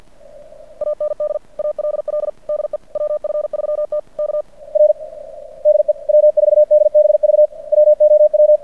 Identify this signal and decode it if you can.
Morse code, heard as a keyed tone, about 50 wpm, 600 Hz. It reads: AND ALL DE WF3T K R DE W4DBV WOW
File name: archived / CRSnet